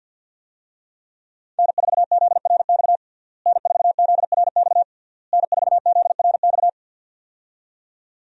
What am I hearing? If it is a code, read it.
Morse code, heard as a keyed tone, about 53 wpm, 700 Hz. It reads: N4ZRX N4ZRX N4ZRX
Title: USA Call Signs - 5 Characters - Repeated 3x - 50wpm